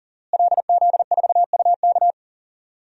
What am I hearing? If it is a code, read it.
Morse code, heard as a keyed tone, about 40 wpm, 700 Hz. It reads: LZ4UK
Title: DXCC-100 Call Signs - 5 Characters - Repeated 3x - 40wpm